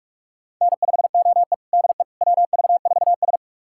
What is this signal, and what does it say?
Morse code, heard as a keyed tone, about 42 wpm, 700 Hz. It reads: N5OE DE WV4S
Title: USA Callsign DE Callsign - 4 Characters - 45wpm